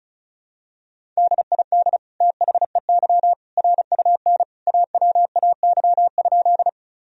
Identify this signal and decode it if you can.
Morse code, heard as a keyed tone, about 36 wpm, 700 Hz. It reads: DID THEY RUN AWAY?
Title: Sentences from Top 1000 Words - Review Part 1 - 35wpm